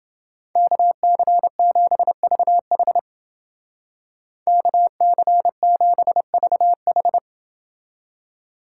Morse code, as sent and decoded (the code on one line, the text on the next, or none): -.- -.-. --... ...- .... / -.- -.-. --... ...- ....
KC7VH KC7VH